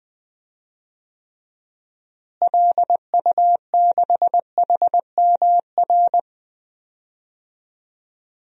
Morse code, heard as a keyed tone, about 20 wpm, 700 Hz. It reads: LU6HMR
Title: DXCC-100 Call Signs - 6 Characters - Repeated 3x - 20wpm